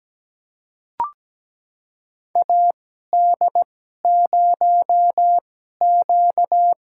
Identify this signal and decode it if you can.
Morse code, heard as a keyed tone, about 17 wpm, spaced as for 12 wpm, 700 Hz. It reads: AD0Q
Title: USA Call Signs - 2x Character Spacing - 17wpm